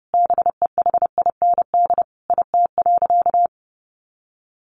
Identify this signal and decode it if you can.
Morse code, heard as a keyed tone, about 28 wpm, 700 Hz. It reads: BEHIND IT.
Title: Sentences from Top 500 Words 30wpm